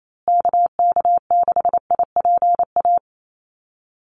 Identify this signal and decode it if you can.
Morse code, heard as a keyed tone, about 28 wpm, 700 Hz. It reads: KK6IPA
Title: USA Call Signs Repeated 2x 28wpm